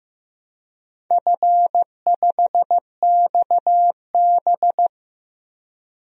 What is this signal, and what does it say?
Morse code, heard as a keyed tone, about 16 wpm, 700 Hz. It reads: F5XB
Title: DXCC-100 Call Signs - 4 Characters - Repeated 3x - 15wpm